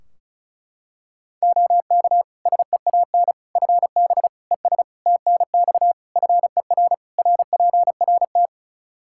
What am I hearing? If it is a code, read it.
Morse code, heard as a keyed tone, about 36 wpm, 700 Hz. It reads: OK SEAN FB ES TNX FER RPRT
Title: Practice QSO - Exchange 2 - Info - 35wpm